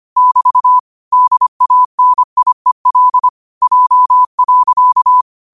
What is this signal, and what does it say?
Morse code, heard as a keyed tone, about 25 wpm, 1000 Hz. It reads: <BT> DANIEL J.
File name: qotdfiles / 25wpm